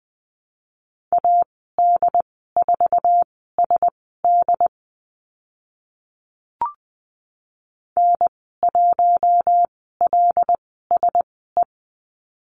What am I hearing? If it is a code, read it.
Morse code, heard as a keyed tone, about 20 wpm, spaced as for 15 wpm, 700 Hz. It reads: AD4SD N1LSE
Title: 076 Lesson 13g - Letter L - Call Signs 20wpm